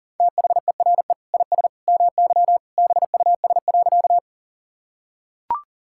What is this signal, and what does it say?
Morse code, heard as a keyed tone, about 40 wpm, 690 Hz. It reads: THERE IS MY BUS.
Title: Sentences from Top 1000 Words - Review Part 6 - 40wpm